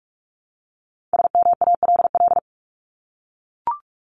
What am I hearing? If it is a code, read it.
Morse code, heard as a keyed tone, about 44 wpm, 700 Hz. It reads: SMALL
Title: Top 200 Words 45wpm